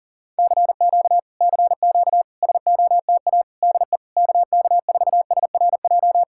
Morse code, heard as a keyed tone, about 40 wpm, 700 Hz. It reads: CQ CQ SOTA DE KK4SRJ